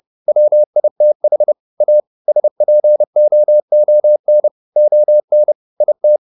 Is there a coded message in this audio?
Yes: Morse code from 0.3 to 6.2 s, heard as a keyed tone, about 30 wpm, 600 Hz. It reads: WITH A SPOON ON IT